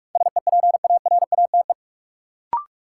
Morse code, heard as a keyed tone, about 46 wpm, 700 Hz. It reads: SEPARATE
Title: Sentences from Top 1000 Words - Review Part 7 - New Word Drill - 45wpm